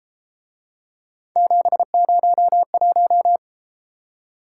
Morse code, heard as a keyed tone, about 33 wpm, 700 Hz. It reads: 701